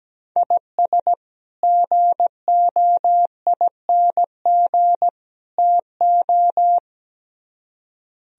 Morse code, heard as a keyed tone, about 17 wpm, 700 Hz. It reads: IS GOING TO
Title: Speed-Racing - Sets of 3 Words 17wpm